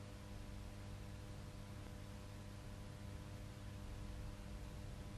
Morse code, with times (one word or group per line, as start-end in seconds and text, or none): none